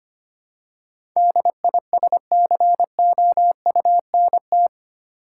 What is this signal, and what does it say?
Morse code, heard as a keyed tone, about 25 wpm, 700 Hz. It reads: DISCOUNT
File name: Eight Letter Words - 25wpm